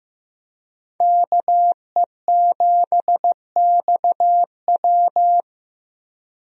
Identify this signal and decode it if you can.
Morse code, heard as a keyed tone, about 16 wpm, 700 Hz. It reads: KE7XW
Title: USA Call Signs - 5 Characters - Repeated 2x - 15wpm